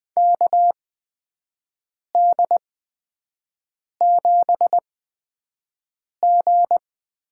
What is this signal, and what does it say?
Morse code, heard as a keyed tone, about 20 wpm, spaced as for 5 wpm, 700 Hz. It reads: KD7G
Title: USA Call Signs - 6 Characters - 8x Character Spacing - 20wpm